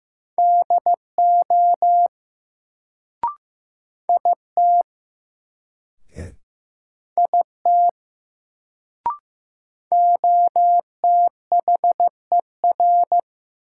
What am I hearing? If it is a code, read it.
Morse code, heard as a keyed tone, about 15 wpm, 700 Hz. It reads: DO IT IT OTHER
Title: Sentences from Top 100 Words - New Word Drill - 15wpm